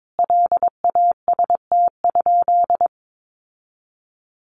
Morse code, heard as a keyed tone, about 22 wpm, 700 Hz. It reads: LAST?
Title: Sentences from Top 1200 Words 22wpm